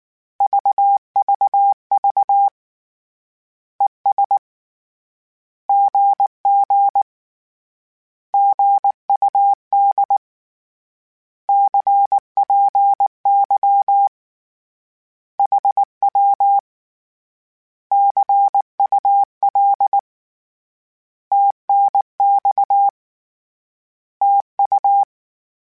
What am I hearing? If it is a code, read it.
Morse code, heard as a keyed tone, about 19 wpm, 800 Hz. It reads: VVV ES GG GUD CPY HW CUL TNX TU
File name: Short Runs / Short Runs-mp3s / Words @18